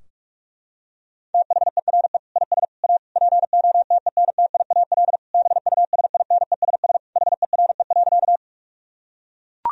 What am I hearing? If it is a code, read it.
Morse code, heard as a keyed tone, about 42 wpm, 700 Hz. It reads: THERE IS A POTENTIAL BUSINESS HERE.
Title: Sentences from Top 1000 Words - Review Part 4 - 45wpm